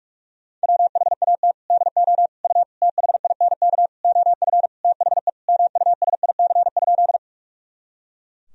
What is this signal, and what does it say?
Morse code, heard as a keyed tone, about 46 wpm, 700 Hz. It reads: WHAT DO U THINK OF THE MUSIC?
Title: Sentences from Top 300 Words and Abbreviations 45wpm